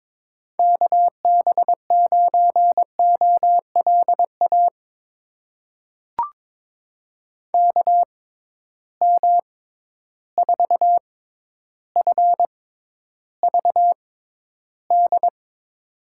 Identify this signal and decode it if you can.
Morse code, heard as a keyed tone, about 22 wpm, spaced as for 11 wpm, 700 Hz. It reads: KB9OLA KM4FVD
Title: USA Call Signs - 6x Character Spacing - 22wpm